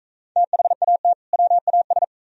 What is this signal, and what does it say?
Morse code, heard as a keyed tone, about 42 wpm, 700 Hz. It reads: THAT WAS